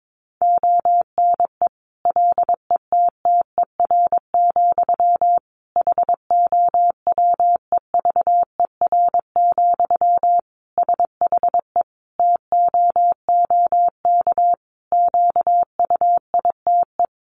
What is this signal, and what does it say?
Morse code, heard as a keyed tone, about 22 wpm, 700 Hz. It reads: ONE LETTER, HOWEVER, SHE TOOK QUITE
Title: Sentences from Top 1200 Words 22wpm